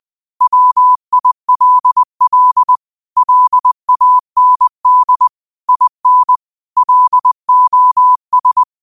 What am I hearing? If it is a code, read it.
Morse code, heard as a keyed tone, about 20 wpm, 1000 Hz. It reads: WILL LAND IN LOS
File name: qotdfiles / 20wpm